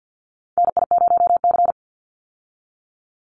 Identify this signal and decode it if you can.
Morse code, heard as a keyed tone, about 49 wpm, 700 Hz. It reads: NI0C
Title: Call Signs Repeated 3x - 50wpm